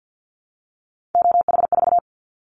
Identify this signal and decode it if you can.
Morse code, heard as a keyed tone, about 50 wpm, 700 Hz. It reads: OH4
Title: Callsign Prefixes for Top 25 DXCC - 50wpm